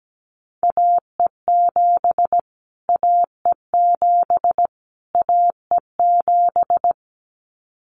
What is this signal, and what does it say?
Morse code, heard as a keyed tone, about 17 wpm, 700 Hz. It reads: AE7 AE7 AE7
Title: Callsign Prefixes for Top 25 DXCC - Repeated 3x - 17wpm